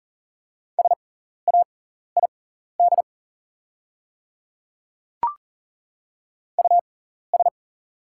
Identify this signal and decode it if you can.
Morse code, heard as a keyed tone, about 40 wpm, spaced as for 14 wpm, 700 Hz. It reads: SAID US
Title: 6x Character Spacing - Top 200 Words - 40wpm